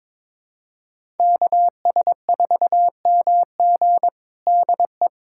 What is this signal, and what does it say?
Morse code, heard as a keyed tone, about 21 wpm, 700 Hz. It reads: KS4MG DE
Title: USA Callsign DE Callsign - 5 Characters - 22wpm